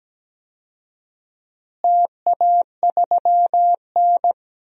Morse code, heard as a keyed tone, about 17 wpm, 700 Hz. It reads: TA3N
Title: DXCC-100 Call Signs - 4 Characters - Repeated 2x - 17wpm